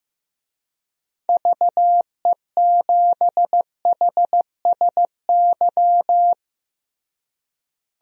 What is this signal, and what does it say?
Morse code, heard as a keyed tone, about 15 wpm, 700 Hz. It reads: VE7HSY
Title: DXCC-100 Call Signs - 6 Characters - Repeated 2x - 15wpm